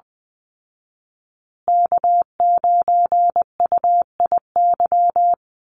Morse code, heard as a keyed tone, about 20 wpm, 700 Hz. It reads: K9UIY
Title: Call Signs Repeated 2x - 20wpm